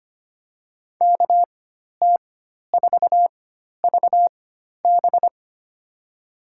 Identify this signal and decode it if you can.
Morse code, heard as a keyed tone, about 25 wpm, spaced as for 12 wpm, 700 Hz. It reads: KT4VB
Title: USA Call Signs - 5 Characters - 4x Character Spacing - 25wpm